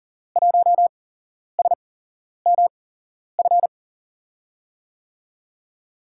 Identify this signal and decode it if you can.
Morse code, heard as a keyed tone, about 40 wpm, spaced as for 11 wpm, 700 Hz. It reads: 1SMF
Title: USA Call Signs - 5 Characters - 8x Character Spacing - 40wpm